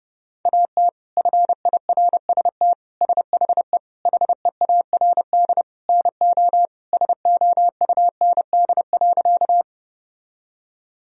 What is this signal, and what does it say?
Morse code, heard as a keyed tone, about 30 wpm, 700 Hz. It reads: AT FIRST SHE HEARD NO SOUND.
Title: Sentences from Top 1000 Words - Review Part 1 - 30wpm